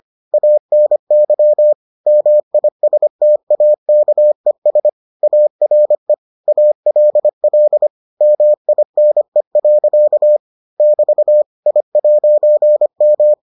Morse code, heard as a keyed tone, about 25 wpm, 600 Hz. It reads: ANY MISTAKES ARE ALL MINE. <BT> I'M